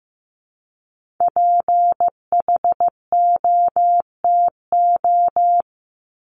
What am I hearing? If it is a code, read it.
Morse code, heard as a keyed tone, about 15 wpm, 700 Hz. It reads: PHOTO
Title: Top 1001-1250 Words 15wpm